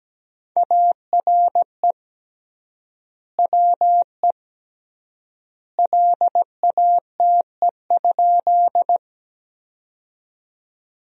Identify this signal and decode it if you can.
Morse code, heard as a keyed tone, about 17 wpm, 700 Hz. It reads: ARE WE LATE?
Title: 3x Word-Spacing - Sentences from Top 300 Words - 17wpm